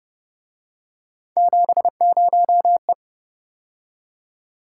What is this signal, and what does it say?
Morse code, heard as a keyed tone, about 28 wpm, 700 Hz. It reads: 70E